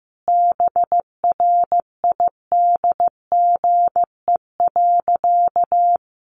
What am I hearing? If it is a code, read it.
Morse code, heard as a keyed tone, about 15 wpm, 700 Hz. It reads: BRIDGE.
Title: Sentences from Top 1500 Words 15wpm